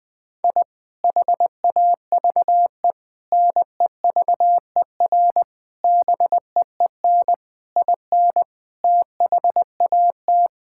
Morse code, heard as a keyed tone, about 21 wpm, 700 Hz. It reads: I HAVE NEVER BEEN IN THAT
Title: Sentences from Top 1000 Words - Review Part 5 - 20wpm